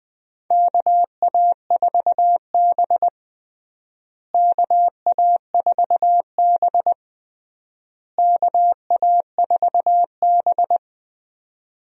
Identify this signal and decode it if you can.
Morse code, heard as a keyed tone, about 21 wpm, 700 Hz. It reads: KA4B KA4B KA4B
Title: USA Call Signs - 4 Characters - Repeated 3x - 20wpm